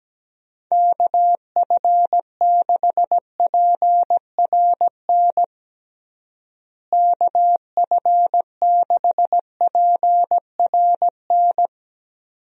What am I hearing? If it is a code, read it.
Morse code, heard as a keyed tone, about 17 wpm, 700 Hz. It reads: KF6PRN KF6PRN
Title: USA Call Signs - 6 Characters - Repeated 2x - 17wpm